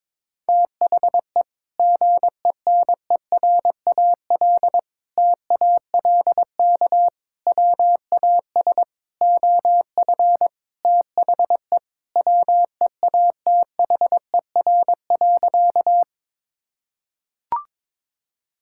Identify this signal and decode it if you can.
Morse code, heard as a keyed tone, about 22 wpm, 700 Hz. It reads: THE GENERAL TALK WAS OF THE WEATHER.
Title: Sentences from Top 1000 Words - Review Part 5 - 22wpm